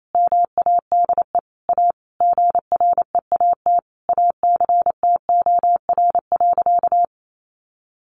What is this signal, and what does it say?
Morse code, heard as a keyed tone, about 28 wpm, 700 Hz. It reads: MADE A GREAT ACTOR.
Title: Sentences from Top 1500 Words 28wpm